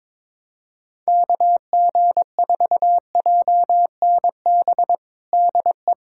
Morse code, heard as a keyed tone, about 22 wpm, 700 Hz. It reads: KG4JNB DE
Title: USA Callsign DE Callsign 22wpm